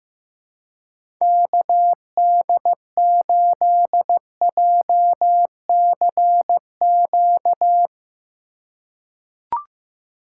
Morse code, heard as a keyed tone, about 15 wpm, 700 Hz. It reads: KD8JCQ
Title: USA Call Signs - 6 Characters - 2x Character Spacing - 15wpm